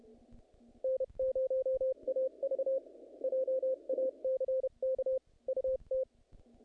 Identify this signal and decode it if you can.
Morse code, heard as a keyed tone, about 30 wpm, 530 Hz. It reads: N0AV JACK UT
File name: contests / naqp-cw-k5tr-op-k5pi- -2000